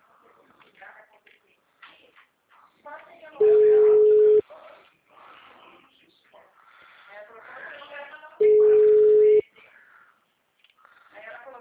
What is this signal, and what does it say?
Ringback tone (Europe, 425 Hz), with speech.
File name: out- -2033- - - .1798